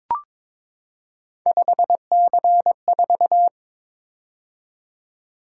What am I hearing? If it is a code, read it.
Morse code, heard as a keyed tone, about 22 wpm, 700 Hz. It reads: 5C4
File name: practice / 22wpm / Three Letter-Number Combinations - 22wpm